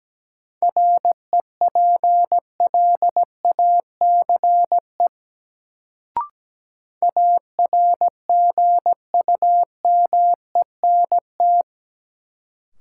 Morse code, heard as a keyed tone, about 17 wpm, 700 Hz. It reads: REPLACE ARGUMENT
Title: Sentences from Top 1400 Words - New Word Drill Alt - 17wpm